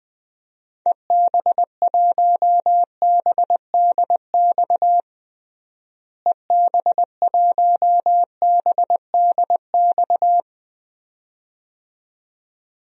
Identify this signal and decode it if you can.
Morse code, heard as a keyed tone, about 21 wpm, 700 Hz. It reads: EB1BDX EB1BDX
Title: DXCC-100 Call Signs - 6 Characters - Repeated 2x - 20wpm